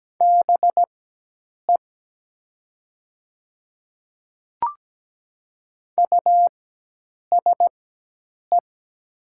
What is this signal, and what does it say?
Morse code, heard as a keyed tone, about 17 wpm, spaced as for 8 wpm, 700 Hz. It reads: BE USE
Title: 4x Character Spacing - Top 300 Words - 17wpm